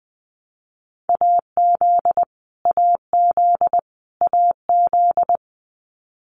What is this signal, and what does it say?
Morse code, heard as a keyed tone, about 20 wpm, 700 Hz. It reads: AZ AZ AZ
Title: US States Repeated 3x 20wpm